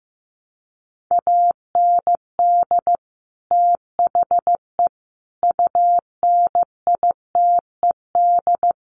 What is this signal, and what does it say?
Morse code, heard as a keyed tone, about 15 wpm, 700 Hz. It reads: AND THE UNITED
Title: Sets of 4 Words 15wpm